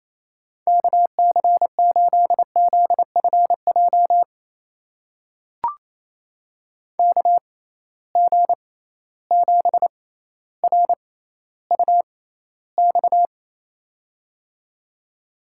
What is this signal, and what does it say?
Morse code, heard as a keyed tone, about 28 wpm, spaced as for 14 wpm, 700 Hz. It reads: KC8ZFJ KG7RUX
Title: USA Call Signs - 6 Characters - 6x Character Spacing - 28wpm